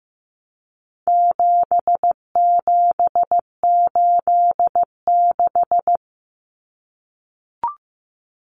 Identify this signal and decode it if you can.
Morse code, heard as a keyed tone, about 15 wpm, 700 Hz. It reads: 7786
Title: Numbers - 4-Digit 15wpm